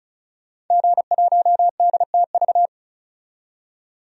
Morse code, heard as a keyed tone, about 34 wpm, 700 Hz. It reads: G1DTV
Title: DXCC-100 Call Signs - 5 Characters - 35wpm